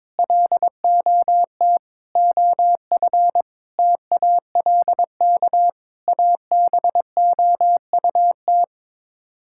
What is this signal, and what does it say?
Morse code, heard as a keyed tone, about 22 wpm, 700 Hz. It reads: LOT OF TALK ABOUT
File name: practice / 22wpm / Sets of 8 Words - 22wpm